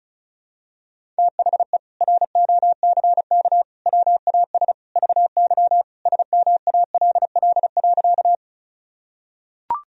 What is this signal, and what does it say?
Morse code, heard as a keyed tone, about 36 wpm, 700 Hz. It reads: THE ROCK WAS VY SMALL.
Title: Sentences from Top 400 Words and Abbreviations 35wpm